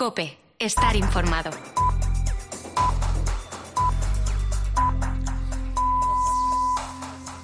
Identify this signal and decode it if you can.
Radio time signal (time pips), with speech. It is heard from 0.8 to 6.8 s.